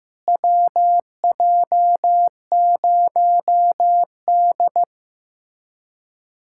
Morse code, heard as a keyed tone, about 15 wpm, 700 Hz. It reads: WJ0D
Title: USA Callsign DE Callsign - 4 Characters - 15wpm